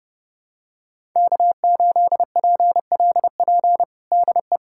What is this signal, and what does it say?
Morse code, heard as a keyed tone, about 30 wpm, 700 Hz. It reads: K8PLP DE
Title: USA Callsign DE Callsign - 5 Characters - 30wpm